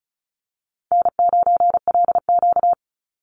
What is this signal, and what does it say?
Morse code, heard as a keyed tone, about 34 wpm, 700 Hz. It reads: N9LQ
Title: Call Signs Repeated 2x - 35wpm